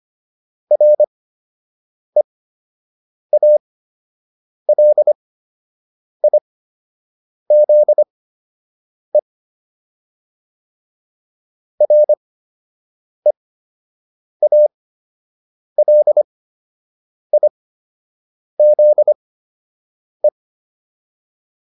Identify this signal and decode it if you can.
Morse code, heard as a keyed tone, about 25 wpm, spaced as for 7 wpm, 600 Hz. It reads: REALIZE REALIZE